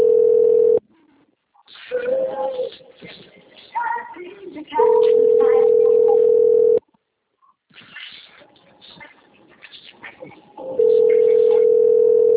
Ringback tone (North America), 440 + 480 Hz.